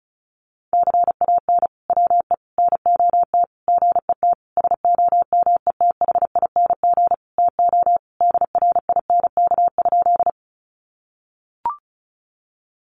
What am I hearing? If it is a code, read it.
Morse code, heard as a keyed tone, about 34 wpm, 700 Hz. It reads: CAN WE NOT GET SOMETHING TO DRINK?
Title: Sentences from Top 1500 Words 35wpm